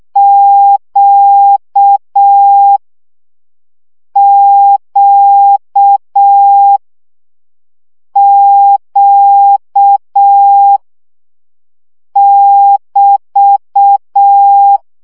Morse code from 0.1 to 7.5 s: Q Q